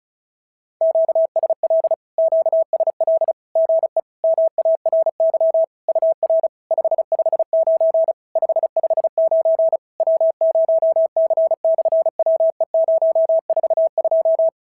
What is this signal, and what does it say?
Morse code, heard as a keyed tone, about 36 wpm, 650 Hz. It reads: QSL QSL GE MARY UR 559 559 W0C/WE042
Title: SOTA QSO Practice 35wpm